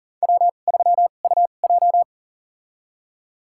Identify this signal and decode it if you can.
Morse code, heard as a keyed tone, about 40 wpm, spaced as for 29 wpm, 700 Hz. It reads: W3UJ
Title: USA Call Signs - 4 Characters - 2x Character Spacing - 40wpm